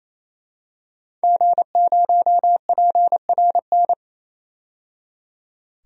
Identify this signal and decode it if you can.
Morse code, heard as a keyed tone, about 28 wpm, 700 Hz. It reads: G0PRN
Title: DXCC-100 Call Signs - 5 Characters - 28wpm